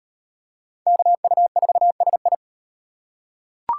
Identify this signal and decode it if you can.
Morse code, heard as a keyed tone, about 38 wpm, 700 Hz. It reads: KU4SI